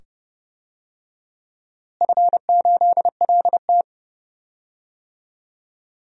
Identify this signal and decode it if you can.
Morse code, heard as a keyed tone, about 30 wpm, 700 Hz. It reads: F8LT
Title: DXCC-100 Call Signs - 4 Characters - Repeated 2x - 30wpm